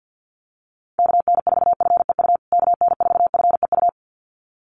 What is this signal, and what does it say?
Morse code, heard as a keyed tone, about 50 wpm, 700 Hz. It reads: KN4FEU KN4FEU
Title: USA Call Signs Repeated 2x 50wpm